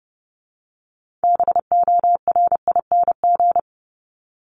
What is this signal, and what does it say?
Morse code, heard as a keyed tone, about 30 wpm, 700 Hz. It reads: BORING
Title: Top 1501-1750 Words 30wpm